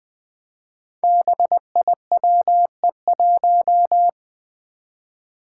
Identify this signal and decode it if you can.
Morse code, heard as a keyed tone, about 20 wpm, 700 Hz. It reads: BIWE1
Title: Five Challenging Letter-Number Combinations 20wpm